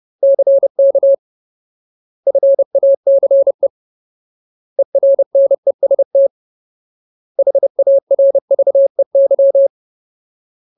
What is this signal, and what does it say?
Morse code, heard as a keyed tone, about 30 wpm, 550 Hz. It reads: CK FACE ERNEST HARVEY
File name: Ultimate Word List / 30WPM Ultimate